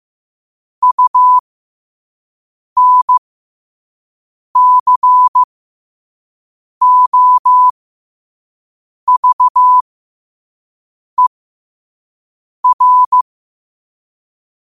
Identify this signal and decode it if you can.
Morse code, heard as a keyed tone, about 15 wpm, spaced as for 5 wpm, 1000 Hz. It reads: UNCOVER
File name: qotdfiles / 5wpm